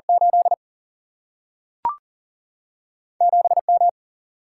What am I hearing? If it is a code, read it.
Morse code, heard as a keyed tone, about 40 wpm, 700 Hz. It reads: 8 7 M
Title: Single-Letter Single-Number 40wpm